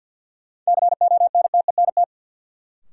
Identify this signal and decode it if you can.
Morse code, heard as a keyed tone, about 53 wpm, 700 Hz. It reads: CONTENT
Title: Sentences from Top 900 Words - New Word Drill - 50wpm